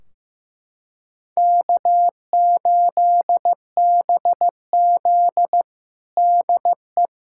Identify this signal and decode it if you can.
Morse code, heard as a keyed tone, about 15 wpm, 700 Hz. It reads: K8BZ DE
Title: K1USN SST QSO Practice 15wpm